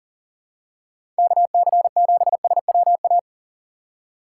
Morse code, heard as a keyed tone, about 40 wpm, 700 Hz. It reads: KC7SWA
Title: USA Call Signs - 6 Characters - 4x Character Spacing - 40wpm